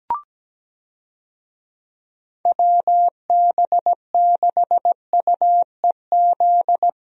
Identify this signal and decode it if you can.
Morse code, heard as a keyed tone, about 17 wpm, 700 Hz. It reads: WB6UEZ
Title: USA Call Signs - 6 Characters - Repeated 2x - 17wpm